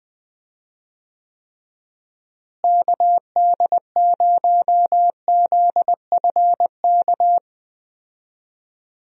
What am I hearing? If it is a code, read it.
Morse code, heard as a keyed tone, about 20 wpm, 700 Hz. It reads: KD0ZFK
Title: USA Call Signs - 6 Characters - Repeated 3x - 20wpm